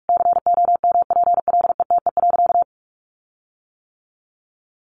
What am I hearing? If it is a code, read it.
Morse code, heard as a keyed tone, about 44 wpm, 700 Hz. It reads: COMPLETE.
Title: Sentences from Top 1300 Words 45wpm